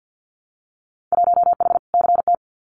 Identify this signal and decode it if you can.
Morse code, heard as a keyed tone, about 50 wpm, 700 Hz. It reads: 1H CT
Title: Winter Field Day Exchanges 50wpm